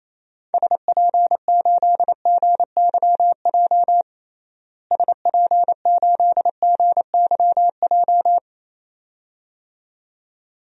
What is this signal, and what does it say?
Morse code, heard as a keyed tone, about 28 wpm, 700 Hz. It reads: SP8GYJ SP8GYJ
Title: DXCC-100 Call Signs - 6 Characters - Repeated 2x - 28wpm